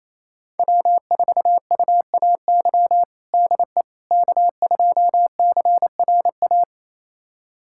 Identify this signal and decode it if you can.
Morse code, heard as a keyed tone, about 28 wpm, 690 Hz. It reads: W4UAY DE K2CRA
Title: USA Callsign DE Callsign - 5 Characters - 28wpm